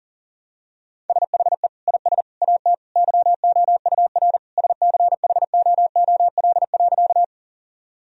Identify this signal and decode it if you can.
Morse code, heard as a keyed tone, about 40 wpm, 700 Hz. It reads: SHE IS AT YOUR SCHOOL.
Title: Sentences from Top 1000 Words - Review Part 1 - 40wpm